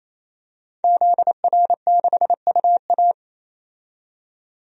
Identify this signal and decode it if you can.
Morse code, heard as a keyed tone, about 28 wpm, 700 Hz. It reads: ZR6UA
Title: DXCC-100 Call Signs - 5 Characters - 28wpm